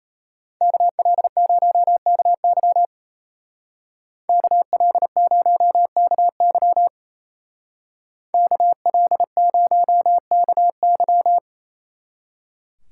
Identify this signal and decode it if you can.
Morse code, heard as a keyed tone, about 32 wpm, 700 Hz. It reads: KL0KY KL0KY KL0KY
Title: Speed Racing - USA 2x2 Call Signs 28wpm